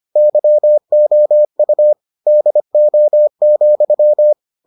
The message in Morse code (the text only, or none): YOU DO,